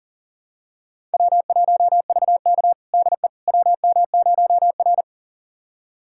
Morse code, heard as a keyed tone, about 40 wpm, 700 Hz. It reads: W1VK DE WM0R
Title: USA Callsign DE Callsign - 4 Characters - 40wpm